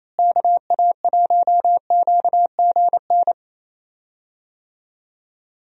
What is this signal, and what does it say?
Morse code, heard as a keyed tone, about 28 wpm, 700 Hz. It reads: KA1QGN
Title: USA Call Signs - 6 Characters - Repeated 3x - 28wpm